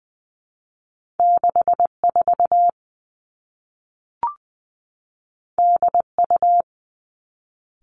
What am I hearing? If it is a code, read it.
Morse code, heard as a keyed tone, about 20 wpm, 700 Hz. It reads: 6 4 D U
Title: Two Challenging Letter-Number Combinations 20wpm